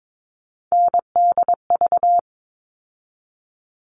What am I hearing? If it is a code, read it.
Morse code, heard as a keyed tone, about 22 wpm, 700 Hz. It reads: NDV
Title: All Cut Numbers - 3-Digit 22wpm